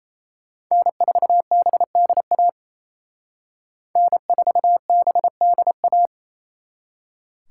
Morse code, heard as a keyed tone, about 36 wpm, 700 Hz. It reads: N4BDA N4BDA